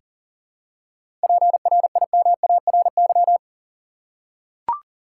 Morse code, heard as a keyed tone, about 40 wpm, 700 Hz. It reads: PRIMARY
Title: Seven Letter Words - 40wpm